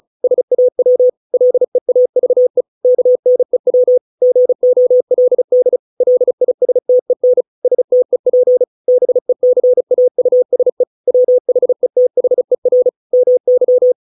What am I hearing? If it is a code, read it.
Morse code, heard as a keyed tone, about 36 wpm, 500 Hz. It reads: SAW LEAVE KNEW GOLD LISTEN STEP BECAUSE WHETHER MY